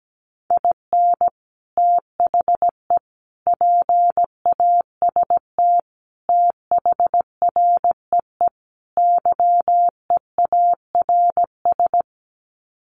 Morse code, heard as a keyed tone, about 17 wpm, 700 Hz. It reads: IN THE PAST THREE YEARS
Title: Sets of 5 Words Encore2 17wpm